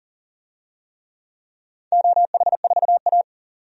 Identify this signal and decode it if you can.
Morse code, heard as a keyed tone, about 40 wpm, 700 Hz. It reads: OH4A
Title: DXCC-100 Call Signs - 4 Characters - 40wpm